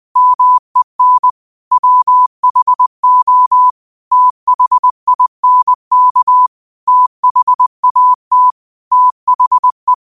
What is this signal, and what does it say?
Morse code, heard as a keyed tone, about 20 wpm, 1000 Hz. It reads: MEN WHO THINK THAT THE
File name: qotdfiles / 20wpm